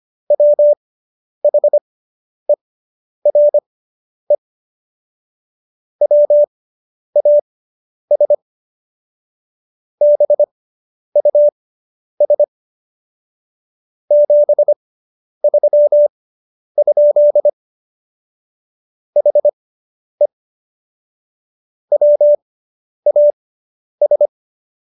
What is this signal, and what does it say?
Morse code, heard as a keyed tone, about 25 wpm, spaced as for 10 wpm, 590 Hz. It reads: WHERE WAS BUS 73? HE WAS